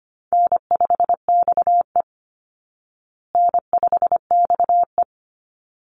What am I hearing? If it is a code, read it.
Morse code, heard as a keyed tone, about 25 wpm, 700 Hz. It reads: N5XE N5XE
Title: Easy Call Signs Repeated 2x - 25wpm